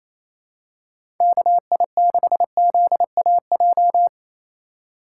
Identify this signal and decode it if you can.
Morse code, heard as a keyed tone, about 28 wpm, 700 Hz. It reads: KI6ZAJ
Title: USA Call Signs - 6 Characters - 3x Character Spacing - 28wpm